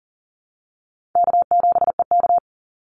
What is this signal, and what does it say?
Morse code, heard as a keyed tone, about 40 wpm, 700 Hz. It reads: K7EK
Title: Easy Call Signs Repeated 3x - 40wpm